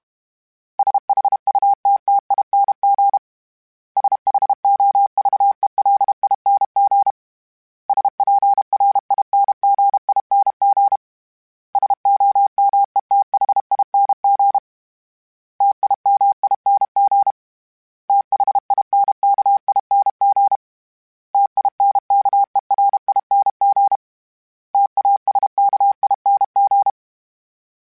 Morse code, heard as a keyed tone, about 32 wpm, 800 Hz. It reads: SHUTTING SHOVELING SPRINGING SOMETHING TIMING THINKING TINKERING TASKING
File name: Special INGs-mp3s / Special INGs-mp3s / Sngl Ing @30